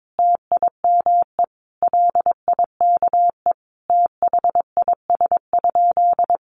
Morse code, heard as a keyed tone, about 22 wpm, 700 Hz. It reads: TIME LIKE THIS?
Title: Sentences from Top 600 Words 22wpm